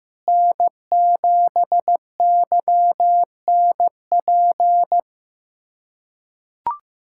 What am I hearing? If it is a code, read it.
Morse code, heard as a keyed tone, about 16 wpm, 700 Hz. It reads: N7YNP
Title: USA Callsign DE Callsign - 5 Characters - 15wpm